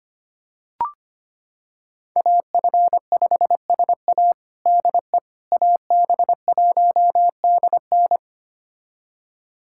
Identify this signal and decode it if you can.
Morse code, heard as a keyed tone, about 25 wpm, 700 Hz. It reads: AF5SA DE AB1DN
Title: USA Callsign DE Callsign - 5 Characters - 25wpm